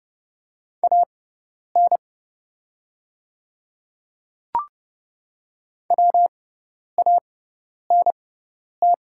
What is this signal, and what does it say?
Morse code, heard as a keyed tone, about 32 wpm, spaced as for 10 wpm, 700 Hz. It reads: AN WANT
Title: 6x Character Spacing - Top 200 Words - 30wpm